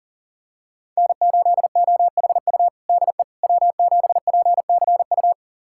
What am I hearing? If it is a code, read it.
Morse code, heard as a keyed tone, about 39 wpm, 700 Hz. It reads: N8OHU DE W7PCU
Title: USA Callsign DE Callsign - 5 Characters - 40wpm